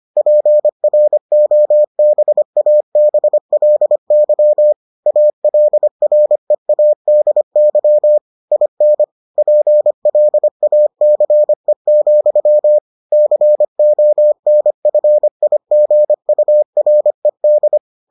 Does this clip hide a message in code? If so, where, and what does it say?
Yes: Morse code from 0.2 to 17.8 s, heard as a keyed tone, about 25 wpm, 600 Hz. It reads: PROBABLY ALREADY IN PLACE, CONFIGURED